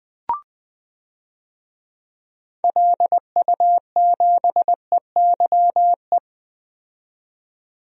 Morse code, heard as a keyed tone, about 20 wpm, 700 Hz. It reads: LU7EYE
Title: DXCC-100 Call Signs - 6 Characters - 20wpm